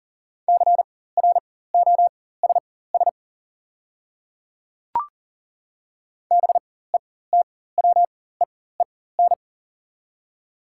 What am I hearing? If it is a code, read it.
Morse code, heard as a keyed tone, about 40 wpm, spaced as for 19 wpm, 690 Hz. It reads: CROSS BETWEEN
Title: 4x Character Spacing - Top 300 Words - 40wpm